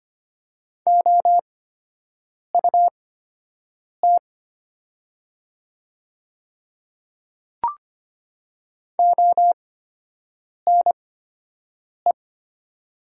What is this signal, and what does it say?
Morse code, heard as a keyed tone, about 25 wpm, spaced as for 7 wpm, 700 Hz. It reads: OUT ONE
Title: 8x Character Spacing - Top 100 Words - 25wpm